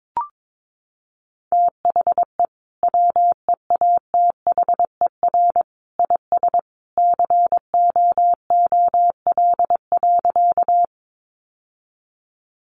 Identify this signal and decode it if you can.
Morse code, heard as a keyed tone, about 22 wpm, 700 Hz. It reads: THE WEATHER IS COOL.
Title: Sentences from Top 1200 Words 22wpm